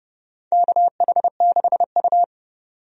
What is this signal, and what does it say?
Morse code, heard as a keyed tone, about 29 wpm, 700 Hz. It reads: KH6U
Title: DXCC-100 Call Signs - 4 Characters - 30wpm